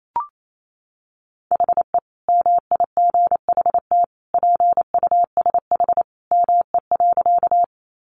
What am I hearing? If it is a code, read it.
Morse code, heard as a keyed tone, about 28 wpm, 700 Hz. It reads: HE MIGHT PUSH ME.
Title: Sentences from Top 1300 Words 28wpm